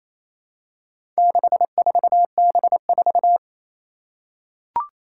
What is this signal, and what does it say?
Morse code, heard as a keyed tone, about 28 wpm, 700 Hz. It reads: 64B4
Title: Four Challenging Letter-Number Combinations 28wpm